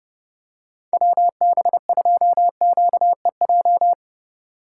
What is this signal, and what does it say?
Morse code, heard as a keyed tone, about 29 wpm, 700 Hz. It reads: WB2QEJ